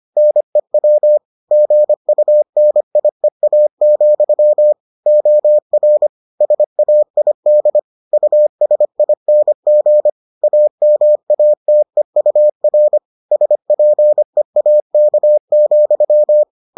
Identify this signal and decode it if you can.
Morse code, heard as a keyed tone, about 25 wpm, 600 Hz. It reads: NEW GUNIEA, OR SAID USING AMATEUR SPEAK,